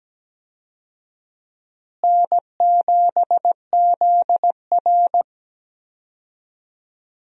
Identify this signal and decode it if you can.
Morse code, heard as a keyed tone, about 17 wpm, 700 Hz. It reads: N7ZR
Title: USA Call Signs - 4 Characters - Repeated 2x - 17wpm